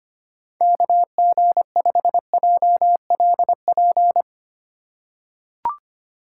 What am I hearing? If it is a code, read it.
Morse code, heard as a keyed tone, about 25 wpm, 700 Hz. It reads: KG5JLP
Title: USA Call Signs - 6 Characters - 4x Character Spacing - 25wpm